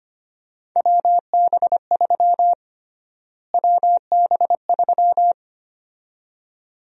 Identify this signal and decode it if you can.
Morse code, heard as a keyed tone, about 25 wpm, 700 Hz. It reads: WB3 WB3
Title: USA Call Sign Prefixes - Repeated 2x - 25wpm